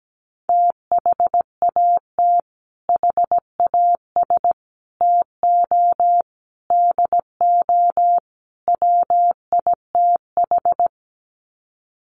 Morse code, heard as a keyed tone, about 17 wpm, 700 Hz. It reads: THAT HAS TO DO WITH
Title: Sets of 5 Words Encore2 17wpm